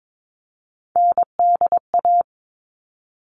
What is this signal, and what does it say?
Morse code, heard as a keyed tone, about 22 wpm, 700 Hz. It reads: NDA
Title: All Cut Numbers - 3-Digit 22wpm